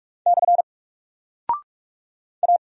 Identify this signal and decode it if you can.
Morse code, heard as a keyed tone, about 44 wpm, 700 Hz. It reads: / A
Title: Single Character (Letter:Number:Punctuation:Prosign) 45wpm